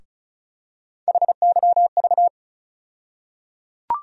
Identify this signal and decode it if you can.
Morse code, heard as a keyed tone, about 34 wpm, 700 Hz. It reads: HYV
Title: USA Call Sign Postfixes - 3 Letters 35wpm